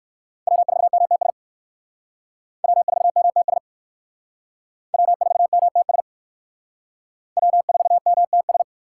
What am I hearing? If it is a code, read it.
Morse code, heard as a keyed tone, about 59 wpm, 700 Hz. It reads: W4MTS W4MTS W4MTS W4MTS